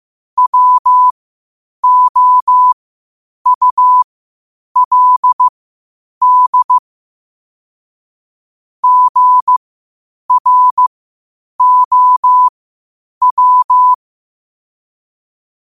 Morse code, heard as a keyed tone, about 15 wpm, spaced as for 8 wpm, 1000 Hz. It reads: WOULD GROW
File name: qotdfiles / 8wpm